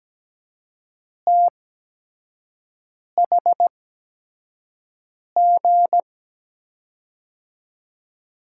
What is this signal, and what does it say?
Morse code, heard as a keyed tone, about 17 wpm, spaced as for 5 wpm, 700 Hz. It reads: THG
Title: USA Call Signs - 6 Characters - 8x Character Spacing - 17wpm